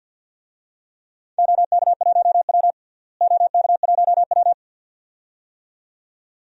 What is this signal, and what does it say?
Morse code, heard as a keyed tone, about 51 wpm, 700 Hz. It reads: OK1W OK1W
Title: DXCC-100 Call Signs - 4 Characters - Repeated 2x - 50wpm